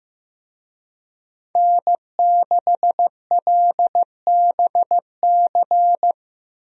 Morse code, heard as a keyed tone, about 15 wpm, 700 Hz. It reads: N6LBC
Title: USA Call Signs - 5 Characters - Repeated 3x - 15wpm